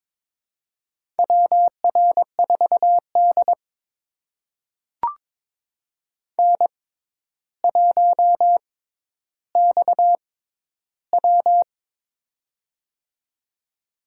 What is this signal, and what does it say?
Morse code, heard as a keyed tone, about 23 wpm, spaced as for 11 wpm, 700 Hz. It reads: WR4D N1XW
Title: USA Call Signs - 4 Characters - 6x Character Spacing - 22wpm